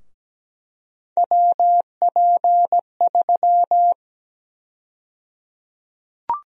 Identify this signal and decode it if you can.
Morse code, heard as a keyed tone, about 17 wpm, 700 Hz. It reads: WP3
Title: USA Call Sign Prefixes - Repeated 2x - 17wpm